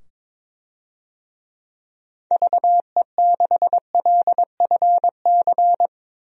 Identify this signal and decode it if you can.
Morse code, heard as a keyed tone, about 22 wpm, 700 Hz. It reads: VE6LFC
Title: DXCC-100 Call Signs - 6 Characters - 22wpm